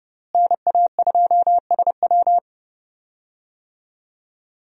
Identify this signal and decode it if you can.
Morse code, heard as a keyed tone, about 30 wpm, 700 Hz. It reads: NA2SW
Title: USA Call Signs - 5 Characters - Repeated 3x - 30wpm